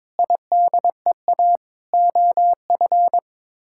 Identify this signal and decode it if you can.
Morse code, heard as a keyed tone, about 22 wpm, 700 Hz. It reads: IDEA OF
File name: practice / 22wpm / Sets of 6 Words - 22wpm